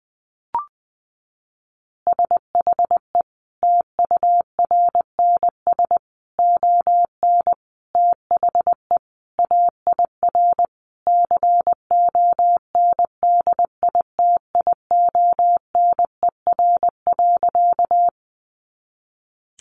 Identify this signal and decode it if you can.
Morse code, heard as a keyed tone, about 20 wpm, 700 Hz. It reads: SHE TURNS ON THE AIR CONDITIONER.
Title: Easy Phrases 20wpm